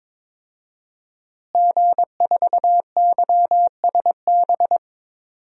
Morse code, heard as a keyed tone, about 22 wpm, 700 Hz. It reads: G4YSB
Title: DXCC-100 Call Signs - 5 Characters - Repeated 2x - 22wpm